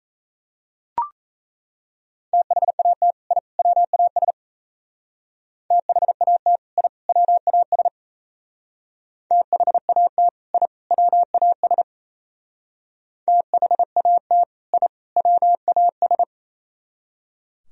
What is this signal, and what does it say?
Morse code, heard as a keyed tone, about 35 wpm, 700 Hz. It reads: THAT I WAS THAT I WAS THAT I WAS THAT I WAS